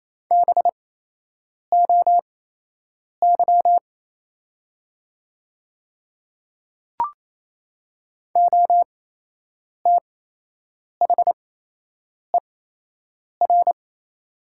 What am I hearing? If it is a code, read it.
Morse code, heard as a keyed tone, about 27 wpm, spaced as for 8 wpm, 700 Hz. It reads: BOY OTHER
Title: 8x Character Spacing - Top 200 Words - 28wpm